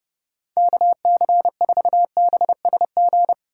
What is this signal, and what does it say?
Morse code, heard as a keyed tone, about 30 wpm, 700 Hz. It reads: KC4BSG
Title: USA Call Signs - 6 Characters - 4x Character Spacing - 30wpm